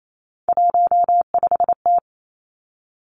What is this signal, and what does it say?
Morse code, heard as a keyed tone, about 28 wpm, 700 Hz. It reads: 15T
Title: Three Challenging Letter-Number Combinations 28wpm